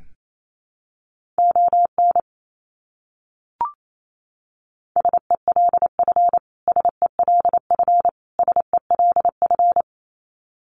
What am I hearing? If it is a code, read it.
Morse code, heard as a keyed tone, about 28 wpm, 700 Hz. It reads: ON SELF SELF SELF
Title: Top 300 Words Repeated 28wpm